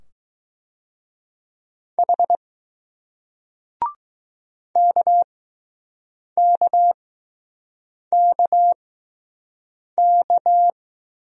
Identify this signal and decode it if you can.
Morse code, heard as a keyed tone, about 20 wpm, spaced as for 6 wpm, 700 Hz. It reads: H KKKK